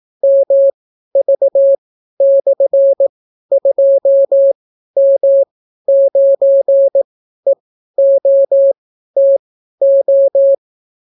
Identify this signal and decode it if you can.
Morse code, heard as a keyed tone, about 18 wpm, spaced as for 12 wpm, 550 Hz. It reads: MV/2M9EOTO